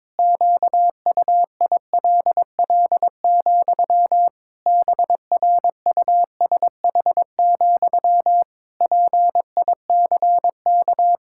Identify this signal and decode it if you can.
Morse code, heard as a keyed tone, about 22 wpm, 700 Hz. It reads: QUILL, BRUSH, PICK
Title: Pangram Challenge v2 - 22wpm